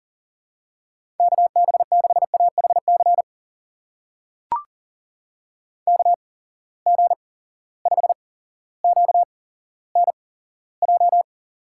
Morse code, heard as a keyed tone, about 40 wpm, spaced as for 17 wpm, 700 Hz. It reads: KB6AHC KG5QNJ
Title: USA Call Signs - 6 Characters - 8x Character Spacing - 40wpm